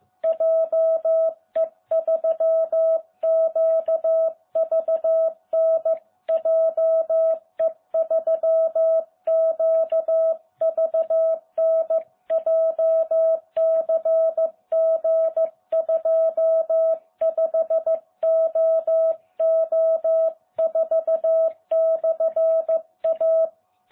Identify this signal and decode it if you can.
Morse code, heard as a keyed tone, about 15 wpm, 660 Hz. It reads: JE3QVN JE3QVN JCG25OO4/A